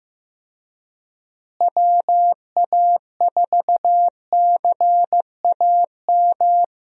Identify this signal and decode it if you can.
Morse code, heard as a keyed tone, about 15 wpm, 700 Hz. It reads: WA4CAM
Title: USA Call Signs - 6 Characters - Repeated 2x - 15wpm